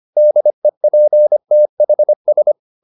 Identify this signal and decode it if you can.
Morse code, heard as a keyed tone, about 25 wpm, 600 Hz. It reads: DEPTHS